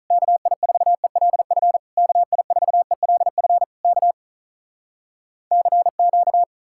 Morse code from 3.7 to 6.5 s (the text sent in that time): K CQ